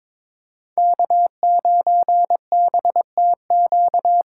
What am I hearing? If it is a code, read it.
Morse code, heard as a keyed tone, about 22 wpm, 700 Hz. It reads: K9BTQ
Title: K1USN SST QSO Practice - 22wpm